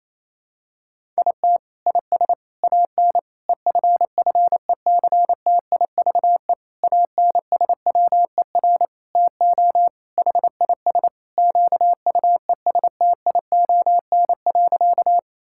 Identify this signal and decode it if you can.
Morse code, heard as a keyed tone, about 28 wpm, 700 Hz. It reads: IT IS AN EFFECTIVE ANSWER TO HIS QUESTION.
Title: Sentences from Top 1000 Words - Review Part 6 - 28wpm